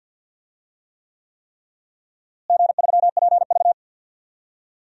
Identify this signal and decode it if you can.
Morse code, heard as a keyed tone, about 51 wpm, 700 Hz. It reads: G3PV
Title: DXCC-100 Call Signs - 4 Characters - Repeated 3x - 50wpm